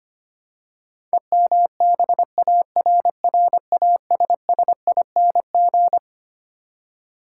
Morse code, heard as a keyed tone, about 25 wpm, 700 Hz. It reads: EMBARRASSING
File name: practice / 25wpm / Twelve Letter Words - 25wpm